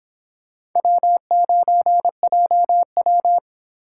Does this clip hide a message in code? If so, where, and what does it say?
Yes: Morse code from 0.8 to 3.4 s, heard as a keyed tone, about 26 wpm, 700 Hz. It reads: W9JW